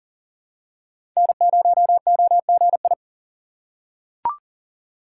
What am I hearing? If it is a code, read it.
Morse code, heard as a keyed tone, about 40 wpm, 700 Hz. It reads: N0OGI
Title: USA Call Signs - 5 Characters - 2x Character Spacing - 40wpm